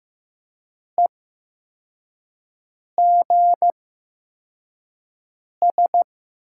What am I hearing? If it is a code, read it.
Morse code, heard as a keyed tone, about 15 wpm, spaced as for 4 wpm, 700 Hz. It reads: EGS
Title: USA Call Signs - 6 Characters - 8x Character Spacing - 15wpm